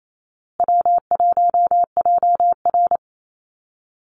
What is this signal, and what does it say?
Morse code, heard as a keyed tone, about 28 wpm, 700 Hz. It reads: W1JR
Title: Call Signs Repeated 2x - 28wpm